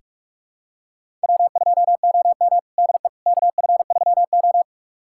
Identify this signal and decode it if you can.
Morse code, heard as a keyed tone, about 41 wpm, 700 Hz. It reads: W2OM DE KF3O
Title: USA Callsign DE Callsign - 4 Characters - 45wpm